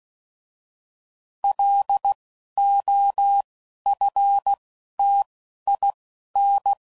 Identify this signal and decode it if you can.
Morse code, heard as a keyed tone, about 16 wpm, 800 Hz. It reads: L O F T I N